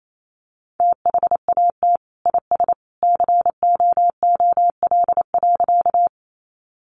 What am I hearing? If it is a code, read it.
Morse code, heard as a keyed tone, about 28 wpm, 700 Hz. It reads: THAT IS COOL.
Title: Sentences from Top 1200 Words 28wpm